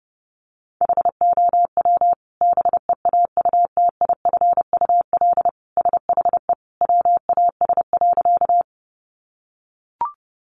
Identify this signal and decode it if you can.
Morse code, heard as a keyed tone, about 30 wpm, 700 Hz. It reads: HOW BEAUTIFUL SHE WAS.
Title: Sentences from Top 900 Words 30wpm